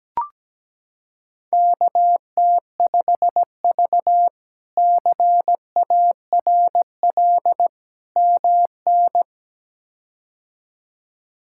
Morse code, heard as a keyed tone, about 17 wpm, 700 Hz. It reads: KT5V CARL MN
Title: K1USN SST QSO Practice - 17wpm